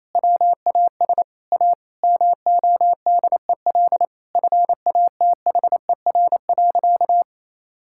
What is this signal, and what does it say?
Morse code, heard as a keyed tone, about 28 wpm, 700 Hz. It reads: WAS A MODEL FATHER.
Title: Sentences from Top 1000 Words - Review Part 5 - 28wpm